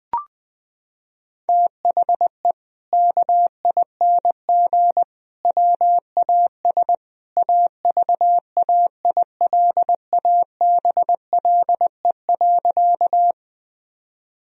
Morse code, heard as a keyed tone, about 21 wpm, 700 Hz. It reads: THE KING WAS AVAILABLE.
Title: Sentences from Top 1000 Words - Review Part 3 - 20wpm